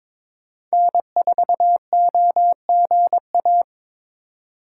Morse code, heard as a keyed tone, about 22 wpm, 700 Hz. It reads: N4OGA
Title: USA Call Signs - 5 Characters - 3x Character Spacing - 22wpm